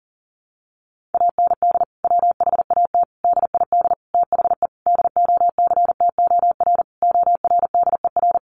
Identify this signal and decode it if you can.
Morse code, heard as a keyed tone, about 40 wpm, 700 Hz. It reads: AND WHAT DID THE DOCTOR ORDER
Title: Sentences from Top 1500 Words 40wpm